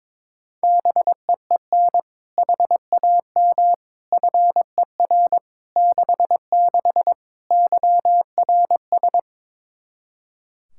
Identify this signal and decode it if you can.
Morse code, heard as a keyed tone, about 22 wpm, 700 Hz. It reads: BEEN HAM FER 66 YRS
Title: Practice QSO - Exchange 3 - More Info - 22wpm